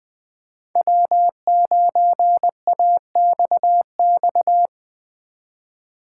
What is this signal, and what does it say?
Morse code, heard as a keyed tone, about 20 wpm, 700 Hz. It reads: W9AXX
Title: Speed Racing - USA Call Signs 17wpm